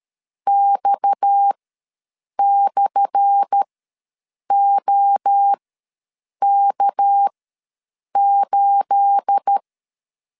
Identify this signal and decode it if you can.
Morse code, heard as a keyed tone, about 13 wpm, spaced as for 7 wpm, 800 Hz. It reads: X/OK8
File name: LEZIONI / 10 SIGNS / 10C DOT ws 12wpm 3548